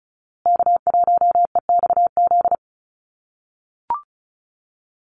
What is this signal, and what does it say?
Morse code, heard as a keyed tone, about 34 wpm, 700 Hz. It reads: K1EXZ
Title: USA Call Signs Repeated 2x 35wpm